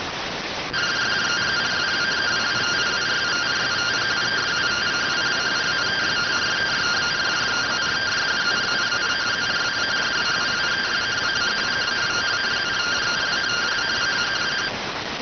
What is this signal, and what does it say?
Two-tone teleprinter signal, about 170 Hz shift.